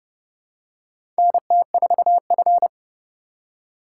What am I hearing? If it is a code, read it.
Morse code, heard as a keyed tone, about 29 wpm, 700 Hz. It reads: NT4F
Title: USA Call Signs - 4 Characters - 2x Character Spacing - 30wpm